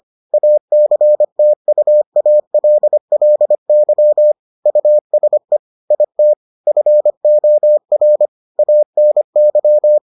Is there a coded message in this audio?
Yes: Morse code from 0.3 to 10.0 s, heard as a keyed tone, about 25 wpm, 600 Hz. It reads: ACTUALLY USE IT FOR ANY